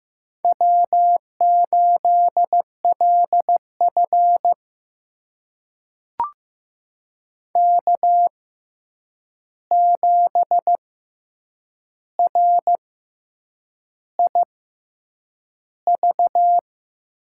Morse code, heard as a keyed tone, about 15 wpm, spaced as for 7 wpm, 700 Hz. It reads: W8LF K7RIV
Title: USA Call Signs - 6x Character Spacing - 15wpm